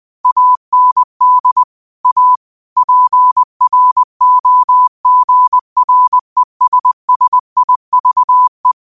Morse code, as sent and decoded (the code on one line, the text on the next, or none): .- -. -.. / .- / .--. .-. --- --. .-. ... ... .. ...- .
AND A PROGRESSIVE